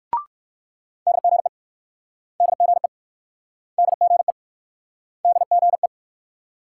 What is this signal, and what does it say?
Morse code, heard as a keyed tone, about 66 wpm, 700 Hz. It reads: DGE DGE DGE DGE